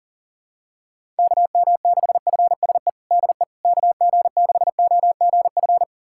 Morse code, heard as a keyed tone, about 40 wpm, 700 Hz. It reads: KM6FSE DE KG6OGF
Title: USA Callsign DE Callsign - 6 Characters - 40wpm